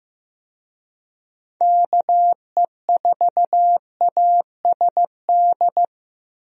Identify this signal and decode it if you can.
Morse code, heard as a keyed tone, about 15 wpm, 700 Hz. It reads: KE4ASD
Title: USA Call Signs - 6 Characters - Repeated 2x - 15wpm